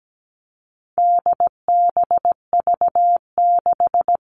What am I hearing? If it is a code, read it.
Morse code, heard as a keyed tone, about 17 wpm, 700 Hz. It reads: DBV6
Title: All Cut Numbers - 4-Digit 17wpm